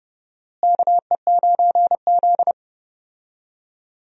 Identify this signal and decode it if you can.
Morse code, heard as a keyed tone, about 30 wpm, 700 Hz. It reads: KE9Z
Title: USA Call Signs - 4 Characters - Repeated 2x - 30wpm